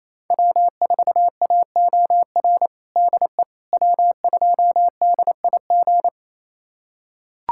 Morse code, heard as a keyed tone, about 28 wpm, 700 Hz. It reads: W4AOR DE W2DIG
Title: USA Callsign DE Callsign - 5 Characters - 28wpm